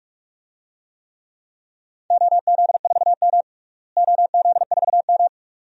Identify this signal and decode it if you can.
Morse code, heard as a keyed tone, about 46 wpm, 700 Hz. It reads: OZ4M OZ4M
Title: DXCC-100 Call Signs - 4 Characters - Repeated 2x - 45wpm